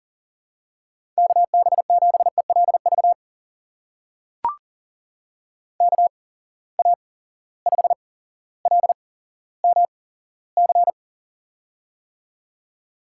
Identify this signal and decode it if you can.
Morse code, heard as a keyed tone, about 41 wpm, spaced as for 17 wpm, 700 Hz. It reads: KB7ELV KA5LMC
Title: USA Call Signs - 6 Characters - 8x Character Spacing - 40wpm